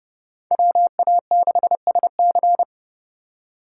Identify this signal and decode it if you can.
Morse code, heard as a keyed tone, about 35 wpm, 700 Hz. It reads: WA6SC
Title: Speed Racing - USA 2x2 Call Signs 30wpm